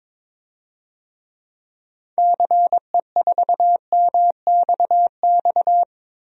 Morse code, heard as a keyed tone, about 22 wpm, 700 Hz. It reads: CE4MXX
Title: DXCC-100 Call Signs - 6 Characters - Repeated 2x - 22wpm